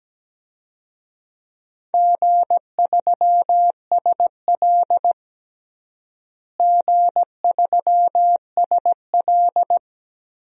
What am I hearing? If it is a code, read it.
Morse code, heard as a keyed tone, about 17 wpm, 700 Hz. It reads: G3SL G3SL
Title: DXCC-100 Call Signs - 4 Characters - Repeated 2x - 17wpm